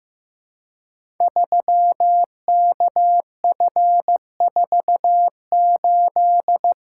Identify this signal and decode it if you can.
Morse code, heard as a keyed tone, about 15 wpm, 700 Hz. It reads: 3KF48
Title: Code Groups - 5 Letter:Numbers - 15wpm